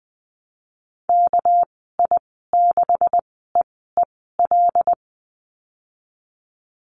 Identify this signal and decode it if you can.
Morse code, heard as a keyed tone, about 20 wpm, spaced as for 15 wpm, 700 Hz. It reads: KI6EEL
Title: 231 Lesson 37g - Letter x - Call Signs 20wpm